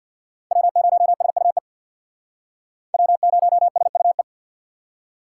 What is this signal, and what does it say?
Morse code, heard as a keyed tone, about 62 wpm, 700 Hz. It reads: W0SUE W0SUE
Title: Speed Racing - USA 1x3 Call Signs 50wpm